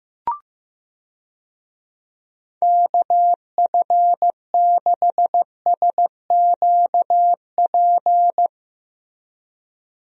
Morse code, heard as a keyed tone, about 15 wpm, 700 Hz. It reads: KF6SQP
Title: USA Call Signs - 6 Characters - Repeated 2x - 15wpm